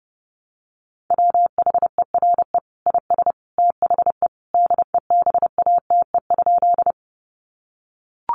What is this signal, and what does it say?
Morse code, heard as a keyed tone, about 30 wpm, 700 Hz. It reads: WHERE IS THE DEBATE?
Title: Sentences from Top 1100 Words 30wpm